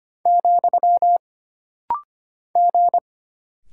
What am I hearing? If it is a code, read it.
Morse code, heard as a keyed tone, about 25 wpm, 700 Hz. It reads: , G
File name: Single Character (Letter:Number:Punctuation:Prosign) - Rapid-Fire - 25wpm